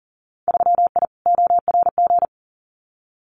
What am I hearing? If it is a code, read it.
Morse code, heard as a keyed tone, about 40 wpm, 700 Hz. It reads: 3I ORG
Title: Winter Field Day Exchanges Encore 40wpm